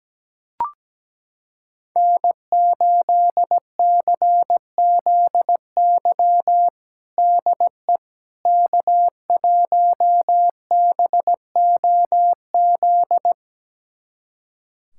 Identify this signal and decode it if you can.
Morse code, heard as a keyed tone, about 17 wpm, 700 Hz. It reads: N8CZY DE K1BOZ
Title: USA Callsign DE Callsign - 5 Characters - 17wpm